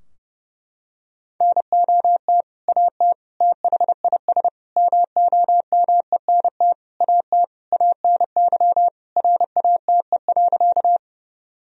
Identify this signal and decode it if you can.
Morse code, heard as a keyed tone, about 30 wpm, 700 Hz. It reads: NOT AT THIS MOMENT AT ANY RATE.
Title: Sentences from Top 1000 Words - Review Part 4 - 30wpm